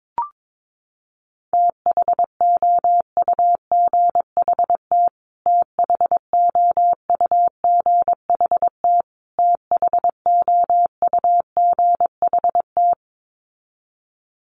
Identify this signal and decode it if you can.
Morse code, heard as a keyed tone, about 22 wpm, 700 Hz. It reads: THOUGHT THOUGHT THOUGHT
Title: Top 300 Words Repeated 22wpm